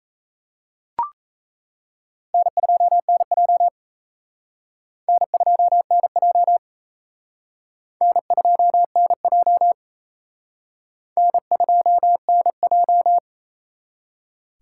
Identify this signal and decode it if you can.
Morse code, heard as a keyed tone, about 35 wpm, 700 Hz. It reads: N2NJ N2NJ N2NJ N2NJ